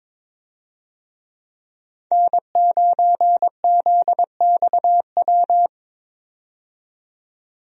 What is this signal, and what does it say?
Morse code, heard as a keyed tone, about 22 wpm, 700 Hz. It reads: N9ZXW
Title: USA Call Signs - 5 Characters - 22wpm